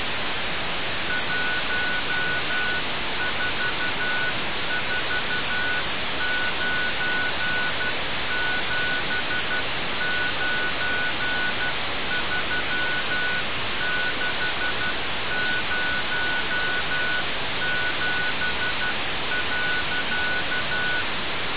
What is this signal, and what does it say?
Morse code, heard as a keyed tone, about 12 wpm, 1480 Hz. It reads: 1449793607.
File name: radiocap / REC z